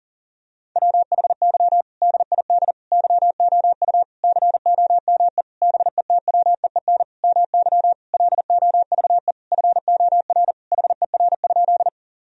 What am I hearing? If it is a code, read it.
Morse code, heard as a keyed tone, about 37 wpm, 700 Hz. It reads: WHY DID YOU COME BETWEEN MY LOVE FOR HER?
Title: Sentences from Top 1000 Words - Review Part 1 - 40wpm